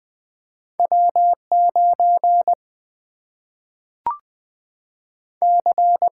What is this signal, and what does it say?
Morse code, heard as a keyed tone, about 20 wpm, 700 Hz. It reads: W9 C
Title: Single-Letter Single-Number 20wpm